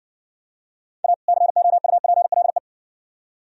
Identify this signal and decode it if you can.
Morse code, heard as a keyed tone, about 60 wpm, 700 Hz. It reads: A COUPLE